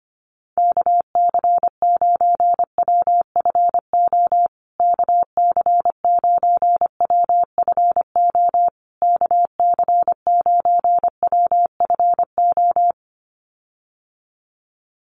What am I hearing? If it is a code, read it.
Morse code, heard as a keyed tone, about 25 wpm, 700 Hz. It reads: KC9WFO KC9WFO KC9WFO
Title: USA Call Signs Repeated 3x 25wpm